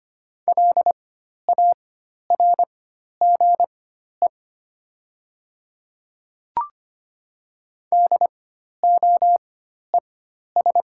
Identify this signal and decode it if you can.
Morse code, heard as a keyed tone, about 25 wpm, spaced as for 12 wpm, 700 Hz. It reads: LARGE DOES
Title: 4x Character Spacing - Top 200 Words - 25wpm